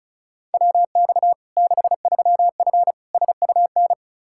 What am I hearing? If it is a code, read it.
Morse code, heard as a keyed tone, about 36 wpm, 700 Hz. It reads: WX 63F SUN
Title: Practice QSO - Exchange 2 - Info - 35wpm